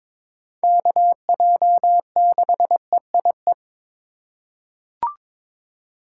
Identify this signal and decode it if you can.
Morse code, heard as a keyed tone, about 22 wpm, 700 Hz. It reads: KJ6EIE
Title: USA Call Signs - 6 Characters - 2x Character Spacing - 22wpm